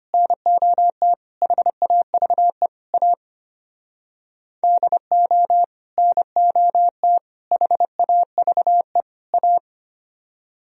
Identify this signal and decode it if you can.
Morse code, heard as a keyed tone, about 32 wpm, 700 Hz. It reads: NOT HAVE A DO NOT HAVE A